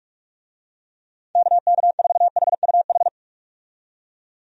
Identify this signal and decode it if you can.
Morse code, heard as a keyed tone, about 41 wpm, 700 Hz. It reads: KK4HUH
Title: USA Call Signs - 6 Characters - 45wpm